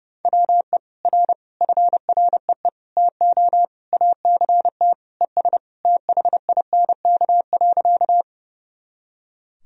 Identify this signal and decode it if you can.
Morse code, heard as a keyed tone, about 32 wpm, 700 Hz. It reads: WE R FREE TO ACT ES THINK.
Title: Sentences from Top 500 Words and Abbreviations 30wpm